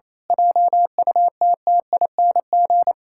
Morse code, heard as a keyed tone, about 27 wpm, 700 Hz. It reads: JUTTING
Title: Pangram Challenge v2 - 28wpm